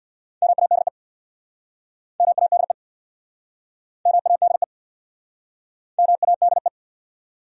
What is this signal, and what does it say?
Morse code, heard as a keyed tone, about 68 wpm, 700 Hz. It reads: MADE MADE MADE MADE